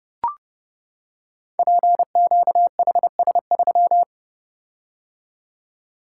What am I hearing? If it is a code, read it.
Morse code, heard as a keyed tone, about 30 wpm, 700 Hz. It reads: PQHS3
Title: Code Groups - 5 Letter:Numbers - 30wpm